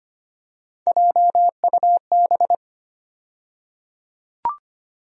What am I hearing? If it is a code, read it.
Morse code, heard as a keyed tone, about 25 wpm, 700 Hz. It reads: JUB
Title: USA Call Sign Postfixes - 3 Letters 25wpm